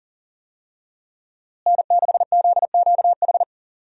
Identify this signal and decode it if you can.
Morse code, heard as a keyed tone, about 38 wpm, 700 Hz. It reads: N6ZQH
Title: USA Call Signs - 5 Characters - Repeated 2x - 40wpm